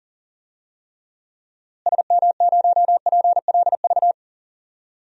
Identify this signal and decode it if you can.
Morse code, heard as a keyed tone, about 40 wpm, 700 Hz. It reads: SM0PLV
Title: DXCC-100 Call Signs - 6 Characters - Repeated 2x - 40wpm